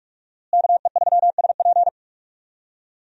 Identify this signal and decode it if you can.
Morse code, heard as a keyed tone, about 41 wpm, 700 Hz. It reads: KE3SP
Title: USA Call Signs - 5 Characters - Repeated 3x - 45wpm